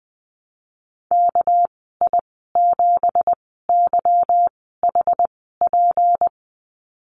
Morse code, heard as a keyed tone, about 20 wpm, spaced as for 15 wpm, 700 Hz. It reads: KI7YHP
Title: 231 Lesson 37g - Letter x - Call Signs 20wpm